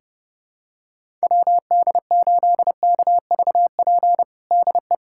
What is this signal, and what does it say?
Morse code, heard as a keyed tone, about 29 wpm, 700 Hz. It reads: WD8KVP DE
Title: USA Callsign DE Callsign - 6 Characters - 30wpm